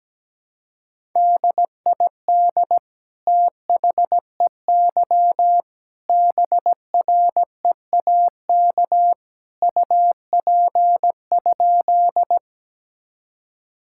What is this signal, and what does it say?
Morse code, heard as a keyed tone, about 17 wpm, 700 Hz. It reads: DID THEY BREAK UP?
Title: Sentences from Top 1000 Words - Review Part 6 - 17wpm